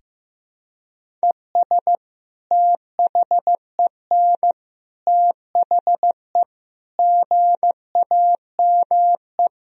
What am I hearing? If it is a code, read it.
Morse code, heard as a keyed tone, about 16 wpm, 700 Hz. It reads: ES THEN THE GAME
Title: Sentences from Top 500 Words and Abbreviations 15wpm